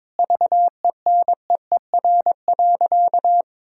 Morse code, heard as a keyed tone, about 22 wpm, 700 Hz. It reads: VENEER.
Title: Pangram Challenge v2 - 22wpm